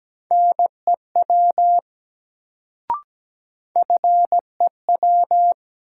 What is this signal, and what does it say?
Morse code, heard as a keyed tone, about 17 wpm, 700 Hz. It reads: NEW FEW
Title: Sets of 2 Words - New Word Drill - 17wpm